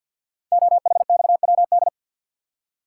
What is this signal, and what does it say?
Morse code, heard as a keyed tone, about 48 wpm, 700 Hz. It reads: OHXWD
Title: Code Groups - 5 Letters - 50wpm